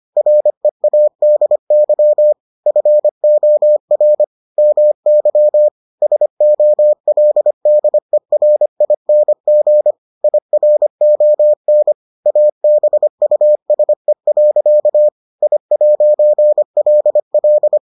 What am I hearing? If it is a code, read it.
Morse code, heard as a keyed tone, about 25 wpm, 600 Hz. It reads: READY FOR MY SOLDERING IRON ABUSE. I'LL